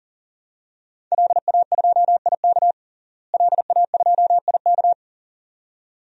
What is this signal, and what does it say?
Morse code, heard as a keyed tone, about 40 wpm, 700 Hz. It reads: LA2IK LA2IK
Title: DXCC-100 Call Signs - 5 Characters - Repeated 2x - 40wpm